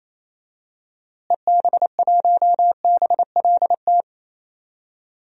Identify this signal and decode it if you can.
Morse code, heard as a keyed tone, about 28 wpm, 700 Hz. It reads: EB1BLT
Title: DXCC-100 Call Signs - 6 Characters - Repeated 3x - 28wpm